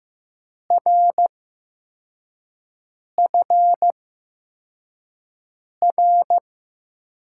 Morse code, heard as a keyed tone, about 15 wpm, spaced as for 4 wpm, 700 Hz. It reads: RFR
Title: USA Call Signs - 8x Character Spacing - 15wpm